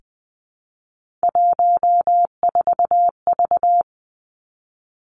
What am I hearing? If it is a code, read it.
Morse code, heard as a keyed tone, about 20 wpm, 700 Hz. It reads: 14V
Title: Three Challenging Letter-Number Combinations 20wpm